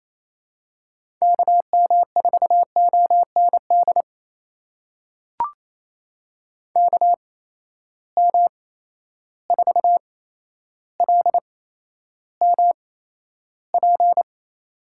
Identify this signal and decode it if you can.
Morse code, heard as a keyed tone, about 28 wpm, spaced as for 12 wpm, 700 Hz. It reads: KM4OND KM4LMP
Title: USA Call Signs - 6 Characters - 8x Character Spacing - 28wpm